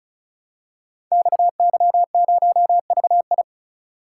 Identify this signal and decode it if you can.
Morse code, heard as a keyed tone, about 36 wpm, 700 Hz. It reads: XY0VI
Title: Code Groups - 5 Letter:Numbers - 35wpm